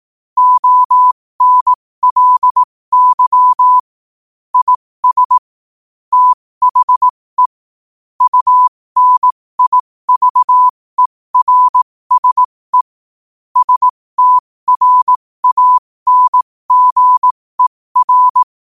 Morse code, heard as a keyed tone, about 18 wpm, spaced as for 15 wpm, 1000 Hz. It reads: ONLY IS THE UNIVERSE STRANGER
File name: qotdfiles / 15wpm